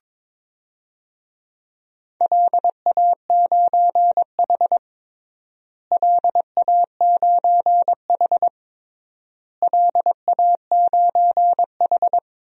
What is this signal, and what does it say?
Morse code, heard as a keyed tone, about 22 wpm, 700 Hz. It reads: LA9H LA9H LA9H
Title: DXCC-100 Call Signs - 4 Characters - Repeated 3x - 22wpm